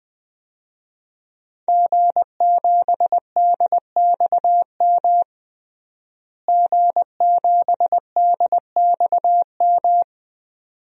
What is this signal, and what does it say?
Morse code, heard as a keyed tone, about 20 wpm, 700 Hz. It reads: G7DXM G7DXM
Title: DXCC-100 Call Signs - 5 Characters - Repeated 2x - 20wpm